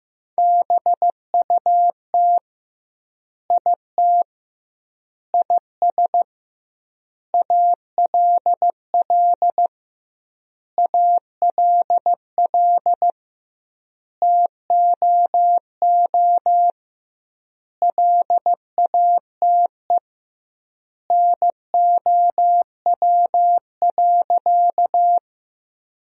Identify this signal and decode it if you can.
Morse code, heard as a keyed tone, about 15 wpm, 700 Hz. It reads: BUT IT IS ALL ALL TOO LATE NOW.
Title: 2x Word-Spacing - Sentences from Top 300 Words - 15wpm